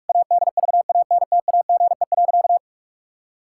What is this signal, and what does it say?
Morse code, heard as a keyed tone, about 46 wpm, 700 Hz. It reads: ADVANTAGE.
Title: Sentences from Top 1000 Words - Review Part 6 - 45wpm